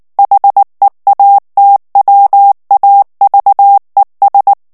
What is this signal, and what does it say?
Morse code, heard as a keyed tone, about 19 wpm, 800 Hz. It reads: HEATWAVES
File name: morse mp3 / ANGLAIS texte clair 2 vitesse 18 mots minute